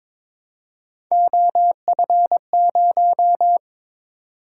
Morse code, heard as a keyed tone, about 23 wpm, 700 Hz. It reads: OF0
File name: practice / 22wpm / Three Letter-Number Combinations - 22wpm